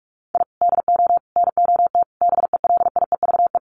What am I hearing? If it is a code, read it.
Morse code, heard as a keyed tone, about 44 wpm, 700 Hz. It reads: I DO NOT BELIEVE
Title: Sets of 5 Words Encore 45wpm